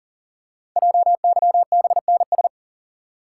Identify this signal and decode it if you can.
Morse code, heard as a keyed tone, about 40 wpm, 700 Hz. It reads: JYBNS
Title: Code Groups - 5 Letters - 40wpm